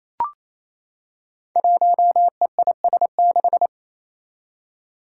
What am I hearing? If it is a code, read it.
Morse code, heard as a keyed tone, about 27 wpm, 700 Hz. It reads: 1EIS6
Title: Five Challenging Letter-Number Combinations 28wpm